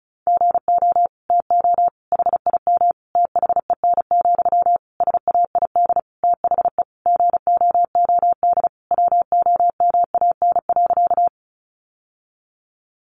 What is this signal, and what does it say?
Morse code, heard as a keyed tone, about 34 wpm, 700 Hz. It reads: GO TO HIM THEN, SAID THE GOOD WOMAN.
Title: Sentences from Top 1100 Words 35wpm